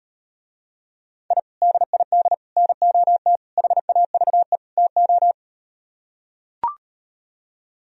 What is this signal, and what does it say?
Morse code, heard as a keyed tone, about 38 wpm, 700 Hz. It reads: I DID NOT HAVE TO